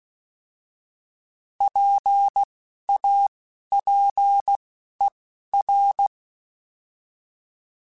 Morse code, heard as a keyed tone, about 16 wpm, 800 Hz. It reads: P A P E R
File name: Alphabetical / Alphabetical mp3s / Ps @10